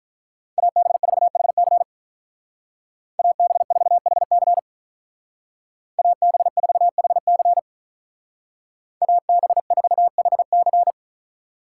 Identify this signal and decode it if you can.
Morse code, heard as a keyed tone, about 42 wpm, 700 Hz. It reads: AB4HC AB4HC AB4HC AB4HC
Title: Speed Racing - USA 2x2 Call Signs 35wpm